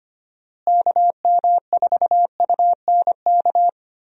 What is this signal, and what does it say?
Morse code, heard as a keyed tone, about 25 wpm, 700 Hz. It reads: KM4UNK
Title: USA Call Signs - 6 Characters - Repeated 2x - 25wpm